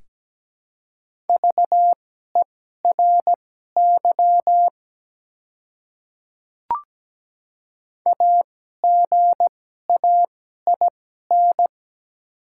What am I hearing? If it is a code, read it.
Morse code, heard as a keyed tone, about 17 wpm, spaced as for 12 wpm, 690 Hz. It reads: VERY AGAIN
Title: 2x Character Spacing - Top 300 Words - 17wpm